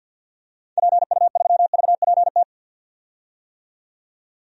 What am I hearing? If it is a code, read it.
Morse code, heard as a keyed tone, about 50 wpm, 700 Hz. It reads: PU3VPT
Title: DXCC-100 Call Signs - 6 Characters - 50wpm